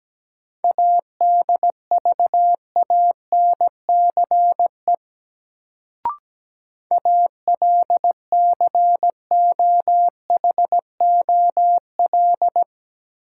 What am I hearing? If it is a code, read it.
Morse code, heard as a keyed tone, about 17 wpm, 700 Hz. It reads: ADVANCE ALCOHOL
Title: Sentences from Top 1400 Words - New Word Drill Alt - 17wpm